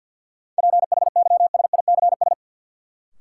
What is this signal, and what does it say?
Morse code, heard as a keyed tone, about 51 wpm, 700 Hz. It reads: PHYSICS
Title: Sentences from Top 1000 Words - Review Part 5 - New Word Drill - 50wpm